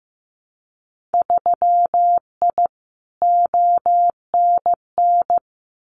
Morse code, heard as a keyed tone, about 15 wpm, 700 Hz. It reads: 3I ONN
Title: Winter Field Day Exchanges Encore 15wpm